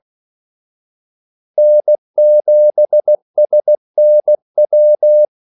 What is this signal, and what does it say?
Morse code, heard as a keyed tone, about 16 wpm, 600 Hz. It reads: N7SNW
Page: Kaldesignaler 16 wpm